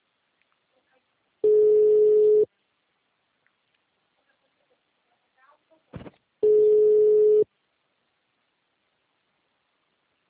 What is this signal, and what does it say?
Ringback tone (Europe, 425 Hz), with speech.